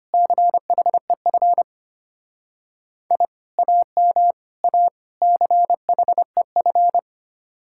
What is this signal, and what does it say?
Morse code, heard as a keyed tone, about 33 wpm, 700 Hz. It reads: CHEF I AM A CHEF